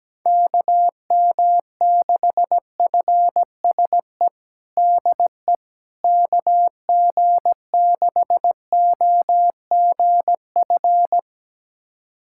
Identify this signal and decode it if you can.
Morse code, heard as a keyed tone, about 17 wpm, 700 Hz. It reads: KM6FSE DE KG6OGF
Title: USA Callsign DE Callsign - 6 Characters - 17wpm